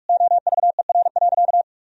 Morse code, heard as a keyed tone, about 46 wpm, 700 Hz. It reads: OVER.
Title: Sentences from Top 1000 Words - Review Part 2 - 45wpm